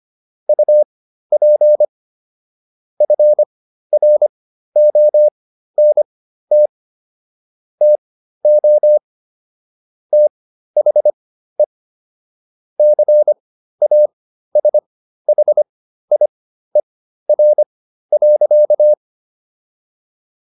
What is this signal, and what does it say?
Morse code, heard as a keyed tone, about 25 wpm, spaced as for 13 wpm, 600 Hz. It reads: UP FRONT TO THE CASHIER.